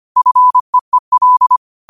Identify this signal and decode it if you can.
Morse code, heard as a keyed tone, about 25 wpm, 1000 Hz. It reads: FEEL
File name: 25wpm